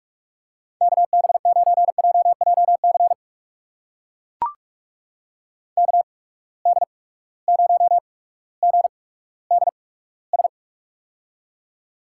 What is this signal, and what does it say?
Morse code, heard as a keyed tone, about 44 wpm, spaced as for 19 wpm, 700 Hz. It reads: KB9JJC KD0GDS
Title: USA Call Signs - 8x Character Spacing - 45wpm